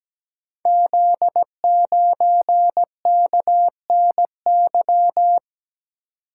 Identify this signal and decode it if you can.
Morse code, heard as a keyed tone, about 17 wpm, 700 Hz. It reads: Z9KNY
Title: Code Groups - 5 Letter:Numbers - 17wpm